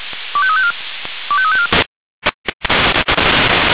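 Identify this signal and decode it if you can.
Five-tone selective call (ZVEI), heard as quick runs of five tones, tones of about 70 ms. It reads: 25354 25354
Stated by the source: ZVEI five-tone selective call received on an SDR.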